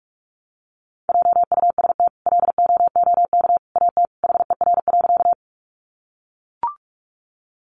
Morse code, heard as a keyed tone, about 44 wpm, 700 Hz. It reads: JUST LOOK AT HER.
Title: Sentences from Top 200 Words 45wpm